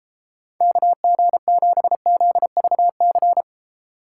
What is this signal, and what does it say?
Morse code, heard as a keyed tone, about 36 wpm, 700 Hz. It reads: KG7ZVC